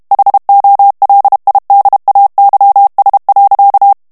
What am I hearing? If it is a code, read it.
Morse code, heard as a keyed tone, about 32 wpm, 800 Hz. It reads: HOLIDAYS.
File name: morse mp3 / ANGLAIS texte clair 1 vitesse 30 mots minute